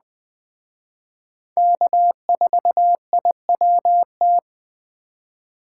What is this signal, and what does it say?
Morse code, heard as a keyed tone, about 21 wpm, 700 Hz. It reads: K4IWT
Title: Speed Racing - USA 1x3 Call Signs 17wpm